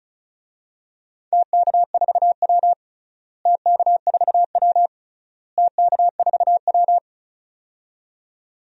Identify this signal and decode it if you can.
Morse code, heard as a keyed tone, about 36 wpm, 700 Hz. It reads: TK4W TK4W TK4W
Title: DXCC-100 Call Signs - 4 Characters - Repeated 3x - 35wpm